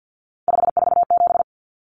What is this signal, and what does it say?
Morse code, heard as a keyed tone, about 50 wpm, 700 Hz. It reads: 547
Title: Common Cut Numbers - 3-Digit 50wpm